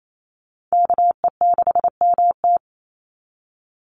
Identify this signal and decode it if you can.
Morse code, heard as a keyed tone, about 28 wpm, 700 Hz. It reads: KE6MT
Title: NA SOTA Chaser Callsigns 28wpm